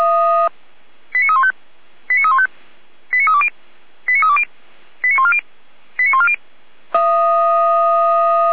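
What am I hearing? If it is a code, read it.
Five-tone selective call (ZVEI), heard as quick runs of five tones, tones of about 70 ms. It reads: 89315 89315 89320 89320 89140 89140
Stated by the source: ZVEI five-tone selective call received by SDR, code 89514.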